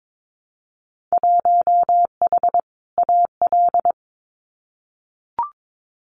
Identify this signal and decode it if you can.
Morse code, heard as a keyed tone, about 22 wpm, 700 Hz. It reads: 1H AL
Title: Winter Field Day Exchanges 22wpm